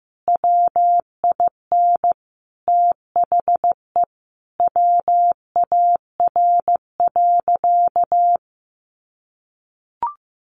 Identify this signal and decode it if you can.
Morse code, heard as a keyed tone, about 15 wpm, 700 Hz. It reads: WIN THE WAR.
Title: Sentences from Top 1500 Words 15wpm